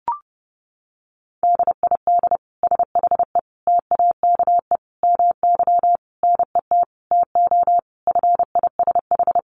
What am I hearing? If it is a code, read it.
Morse code, heard as a keyed tone, about 30 wpm, 700 Hz. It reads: DID SHE TAKE MY NET TO FISH
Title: Sentences from Top 1400 Words 30wpm